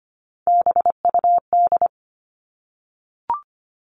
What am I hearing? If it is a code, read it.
Morse code, heard as a keyed tone, about 25 wpm, 700 Hz. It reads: BUD
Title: All Cut Numbers - 3-Digit 25wpm